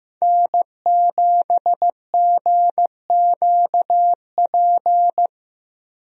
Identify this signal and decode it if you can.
Morse code, heard as a keyed tone, about 15 wpm, 700 Hz. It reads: N7GQP
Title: USA Call Signs - 5 Characters - Repeated 3x - 15wpm